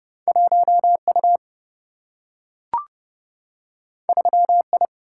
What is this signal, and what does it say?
Morse code, heard as a keyed tone, about 31 wpm, 700 Hz. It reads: 1 U 3 I
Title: Single-Letter Single-Number 30wpm